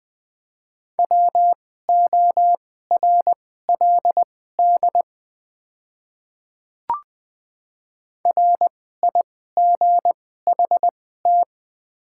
Morse code, heard as a keyed tone, about 20 wpm, spaced as for 14 wpm, 700 Hz. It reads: WORLD RIGHT
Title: 2x Character Spacing - Top 200 Words - 20wpm